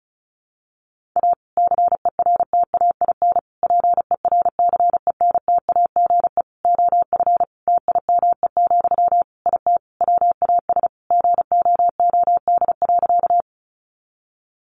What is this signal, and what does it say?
Morse code, heard as a keyed tone, about 34 wpm, 700 Hz. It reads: A CERTAIN PERCENTAGE OF TIME, IT WAS GOOD.
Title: Sentences from Top 1300 Words 35wpm